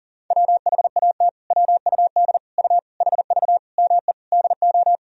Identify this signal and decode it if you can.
Morse code, heard as a keyed tone, about 37 wpm, 700 Hz. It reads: WHAT WUD U HV ME DO
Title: Sentences from Top 300 Words and Abbreviations 40wpm